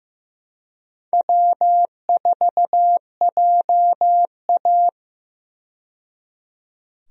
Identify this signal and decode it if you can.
Morse code, heard as a keyed tone, about 16 wpm, 700 Hz. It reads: W4JA
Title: USA Call Signs - 4 Characters - 15wpm